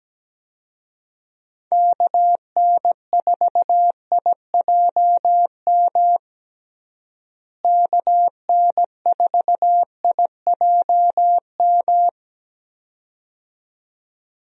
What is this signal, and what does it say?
Morse code, heard as a keyed tone, about 17 wpm, 700 Hz. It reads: KN4IJM KN4IJM
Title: USA Call Signs - 6 Characters - Repeated 2x - 17wpm